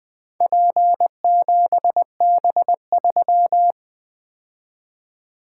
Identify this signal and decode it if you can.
Morse code, heard as a keyed tone, about 20 wpm, 700 Hz. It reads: P7B3
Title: Four Letter-Number Combinations - 20wpm